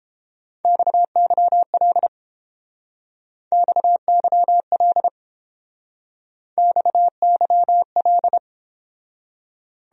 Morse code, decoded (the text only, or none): XYL XYL XYL